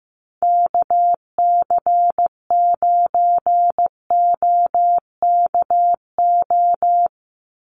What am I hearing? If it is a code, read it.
Morse code, heard as a keyed tone, about 15 wpm, 700 Hz. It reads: KC9OKO
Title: USA Call Signs 15wpm